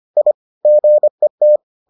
Morse code, heard as a keyed tone, about 25 wpm, 600 Hz. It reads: I GET